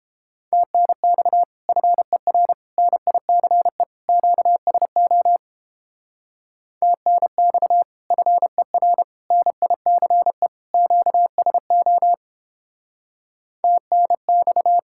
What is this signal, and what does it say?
Morse code, heard as a keyed tone, about 29 wpm, 700 Hz. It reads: TNX FER NICE QSO TNX FER NICE QSO TNX